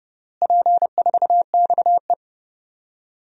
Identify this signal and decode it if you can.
Morse code, heard as a keyed tone, about 30 wpm, 700 Hz. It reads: P4XE
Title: DXCC-100 Call Signs - 4 Characters - Repeated 3x - 30wpm